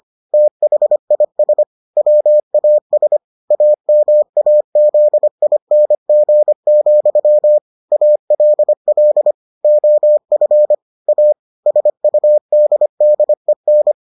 Morse code, heard as a keyed tone, about 25 wpm, 600 Hz. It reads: THIS WAS AMAZING, ALL OF A SUDDEN